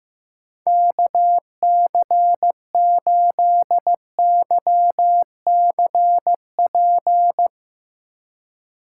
Morse code, heard as a keyed tone, about 15 wpm, 700 Hz. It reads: KC8YCP
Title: USA Call Signs - 6 Characters - 2x Character Spacing - 15wpm